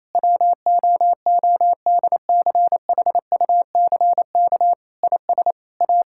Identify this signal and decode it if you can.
Morse code, heard as a keyed tone, about 28 wpm, 700 Hz. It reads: WOODCHUCK IS A
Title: Pangram Challenge v2 - 28wpm